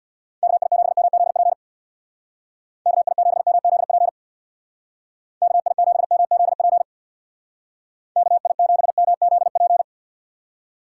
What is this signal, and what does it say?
Morse code, heard as a keyed tone, about 61 wpm, 700 Hz. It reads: KI7MZP KI7MZP KI7MZP KI7MZP
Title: Speed Racing - USA 2x3 Call Signs 50wpm